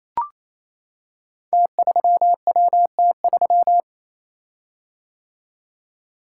Morse code, heard as a keyed tone, about 28 wpm, 690 Hz. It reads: T3WT3
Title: Code Groups - 5 Letter:Numbers - 28wpm